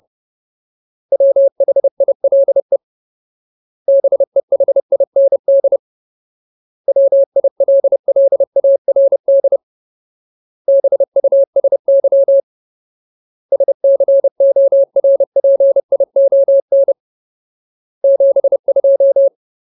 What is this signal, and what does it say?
Morse code, heard as a keyed tone, about 30 wpm, 550 Hz. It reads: WHILE BEHIND WILLARD BUSY SCORPION 72
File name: Ultimate Word List / 30WPM Ultimate